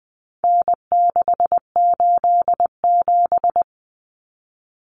Morse code, heard as a keyed tone, about 20 wpm, 700 Hz. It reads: N687
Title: Common Cut Numbers - 4-Digit 20wpm